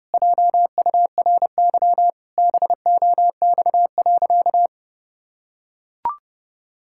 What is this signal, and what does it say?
Morse code, heard as a keyed tone, about 32 wpm, 700 Hz. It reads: JURY BOX.
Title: Pangram Challenge v2 - 30wpm